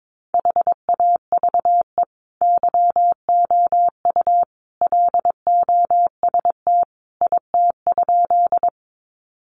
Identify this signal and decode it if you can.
Morse code, heard as a keyed tone, about 22 wpm, 700 Hz. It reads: HAVE YOU LOST IT?
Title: Sentences from Top 1400 Words 22wpm